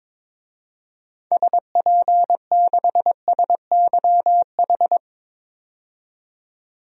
Morse code, heard as a keyed tone, about 22 wpm, 700 Hz. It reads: SP6SYH
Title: DXCC-100 Call Signs - 6 Characters - 22wpm